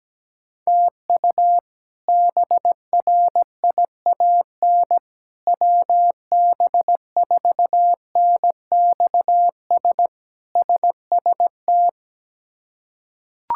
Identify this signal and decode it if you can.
Morse code, heard as a keyed tone, about 17 wpm, 700 Hz. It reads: TU BRIAN WB4NXS SST
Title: K1USN SST QSO Practice - 17wpm